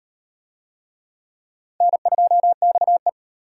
Morse code, heard as a keyed tone, about 34 wpm, 700 Hz. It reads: N2XE